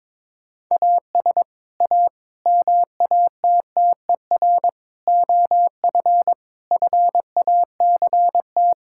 Morse code, heard as a keyed tone, about 22 wpm, 700 Hz. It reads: AS A MATTER OF FACT
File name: practice / 22wpm / Sentences with 4-Word Phrases - 22wpm